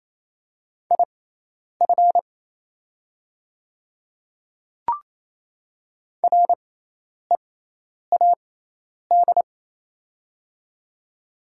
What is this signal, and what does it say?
Morse code, heard as a keyed tone, about 28 wpm, spaced as for 10 wpm, 700 Hz. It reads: IF READ
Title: 6x Character Spacing - Top 500 Words - 28wpm